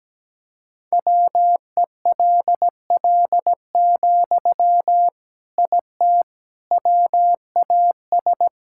Morse code, heard as a keyed tone, about 17 wpm, 700 Hz. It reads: WELL, IT WAS
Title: Sentences from Top 1000 Words - Review Part 6 - 17wpm